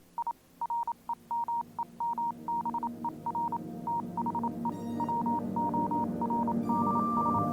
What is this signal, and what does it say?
Morse code, heard as a keyed tone, about 27 wpm, 940 Hz, with noise. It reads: I REMEMBER THE WORDS